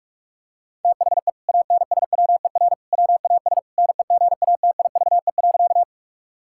Morse code, heard as a keyed tone, about 46 wpm, 700 Hz. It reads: THE ANSWER WAS NEGATIVE.
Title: Sentences from Top 1000 Words - Review Part 6 - 45wpm